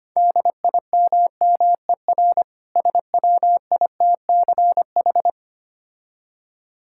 Morse code, heard as a keyed tone, about 25 wpm, 700 Hz. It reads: DIMMER SWITCH
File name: practice / 25wpm / Difficult Words - Speed-Racing with Hints - 25wpm